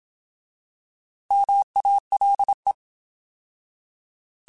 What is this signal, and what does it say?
Morse code, heard as a keyed tone, about 27 wpm, 800 Hz. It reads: MALE
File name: Alphabetical / Alphabetical mp3s / Ms @25